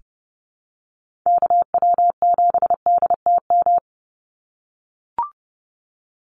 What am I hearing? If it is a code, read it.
Morse code, heard as a keyed tone, about 30 wpm, 700 Hz. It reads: KW7DTM
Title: USA Call Signs 30wpm